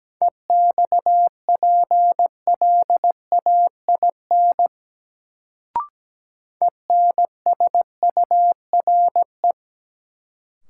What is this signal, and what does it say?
Morse code, heard as a keyed tone, about 17 wpm, 700 Hz. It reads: EXPLAIN ENSURE
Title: Sentences from Top 1100 Words - New Word Drill Alt - 17wpm